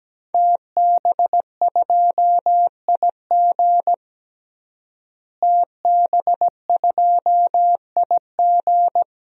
Morse code, heard as a keyed tone, about 17 wpm, 700 Hz. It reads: TB2IG TB2IG
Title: DXCC-100 Call Signs - 5 Characters - Repeated 3x - 17wpm